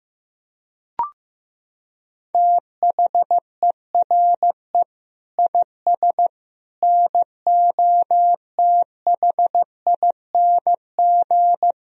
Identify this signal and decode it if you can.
Morse code, heard as a keyed tone, about 15 wpm, 700 Hz. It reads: THERE IS NOTHING
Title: Sentences from Top 500 Words and Abbreviations 15wpm